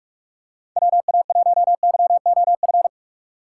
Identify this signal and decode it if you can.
Morse code, heard as a keyed tone, about 47 wpm, 700 Hz. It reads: WA1YOF
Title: USA Call Signs - 6 Characters - 6x Character Spacing - 45wpm